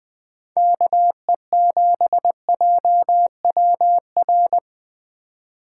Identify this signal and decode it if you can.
Morse code, heard as a keyed tone, about 20 wpm, 700 Hz. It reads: KE7JWR